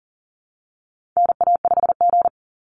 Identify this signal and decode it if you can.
Morse code, heard as a keyed tone, about 40 wpm, 700 Hz. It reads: NA5G
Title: Call Signs Repeated 3x - 40wpm